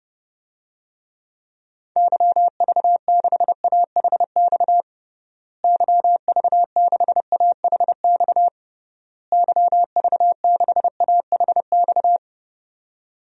Morse code, heard as a keyed tone, about 30 wpm, 700 Hz. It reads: YV6AHX YV6AHX YV6AHX
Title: DXCC-100 Call Signs - 6 Characters - Repeated 3x - 30wpm